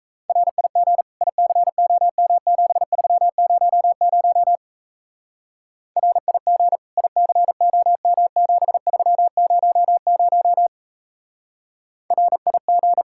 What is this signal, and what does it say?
Morse code, heard as a keyed tone, about 36 wpm, 700 Hz. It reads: RIG ICOM7300 RIG ICOM7300 RIG